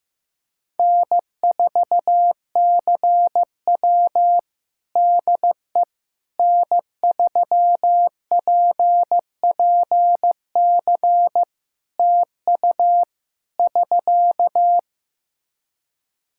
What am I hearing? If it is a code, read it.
Morse code, heard as a keyed tone, about 15 wpm, 700 Hz. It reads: N4CW DE N3PPC TU <SK>
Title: Practice QSO - Exchange 4 - Ending - 15wpm